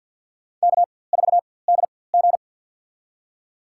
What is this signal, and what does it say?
Morse code, heard as a keyed tone, about 50 wpm, spaced as for 23 wpm, 700 Hz. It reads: K4DG
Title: USA Call Signs - 4x Character Spacing - 50wpm